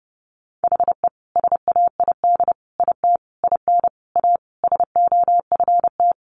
Morse code, heard as a keyed tone, about 30 wpm, 700 Hz. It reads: HE SAID IT IN A SOFT
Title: Sentences from Top 1500 Words 30wpm